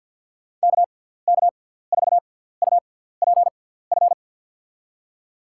Morse code, heard as a keyed tone, about 53 wpm, 700 Hz. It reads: K K 4 U P F
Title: USA Call Signs - 6 Characters - 6x Character Spacing - 50wpm